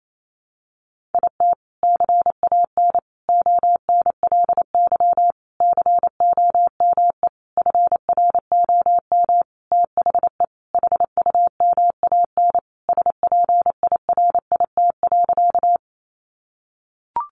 Morse code, heard as a keyed tone, about 28 wpm, 700 Hz. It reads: IT CAN ONLY COME FROM THE HUMAN SPIRIT.
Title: Sentences from Top 1400 Words 28wpm